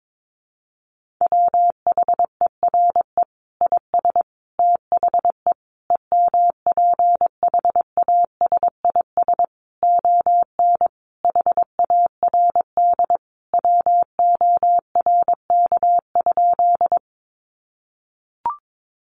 Morse code, heard as a keyed tone, about 22 wpm, 700 Hz. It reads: WHERE IS THE EMPHASIS ON HARD WORK?
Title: Sentences from Top 1400 Words 22wpm